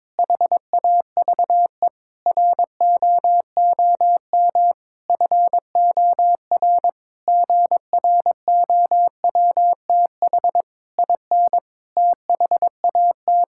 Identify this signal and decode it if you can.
Morse code, heard as a keyed tone, about 22 wpm, 700 Hz. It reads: HAVE ROOM FOR GROWTH IN THAT
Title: Sentences from Top 1000 Words - Review Part 5 - 22wpm